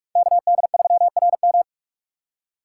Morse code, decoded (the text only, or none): KD3RM